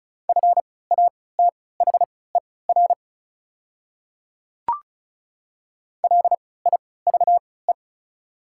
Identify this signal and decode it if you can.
Morse code, heard as a keyed tone, about 36 wpm, spaced as for 20 wpm, 700 Hz. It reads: FATHER LIVE
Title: 3x Character Spacing - Top 200 Words - 35wpm